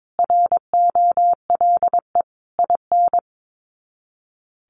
Morse code, heard as a keyed tone, about 22 wpm, 700 Hz. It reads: ROLE IN
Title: Sets of 5 Words Encore2 22wpm